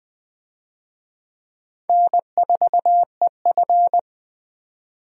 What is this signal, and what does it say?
Morse code, heard as a keyed tone, about 20 wpm, 700 Hz. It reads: N4EF
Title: USA Call Signs - 4 Characters - 20wpm